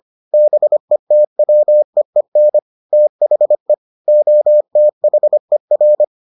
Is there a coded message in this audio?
Yes: Morse code from 0.3 to 6.0 s, heard as a keyed tone, about 25 wpm, 600 Hz. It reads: BETWEEN THE OTHER